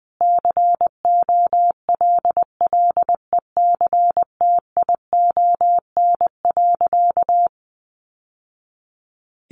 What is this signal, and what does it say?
Morse code, heard as a keyed tone, about 20 wpm, 700 Hz. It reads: COLLECTION.
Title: Sentences from Top 1200 Words 20wpm